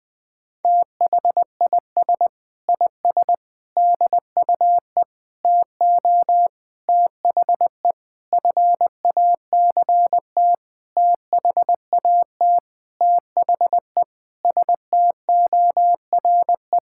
Morse code, heard as a keyed tone, about 20 wpm, 700 Hz. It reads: THIS IS DUE TO THE FACT THAT THE STORE
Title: Sentences with 7-Word Phrases - Spoken Beginning - 20wpm